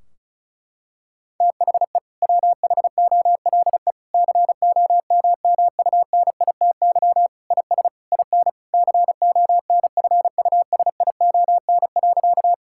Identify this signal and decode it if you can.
Morse code, heard as a keyed tone, about 36 wpm, 700 Hz. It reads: THE WHOLE COMMUNITY IS IN CONFUSION.
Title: Sentences from Top 1600 Words 35wpm